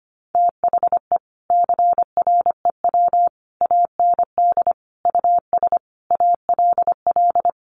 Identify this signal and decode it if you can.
Morse code, heard as a keyed tone, about 25 wpm, 700 Hz. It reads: THE CREW AND US ALL
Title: Sentences from Top 1500 Words 25wpm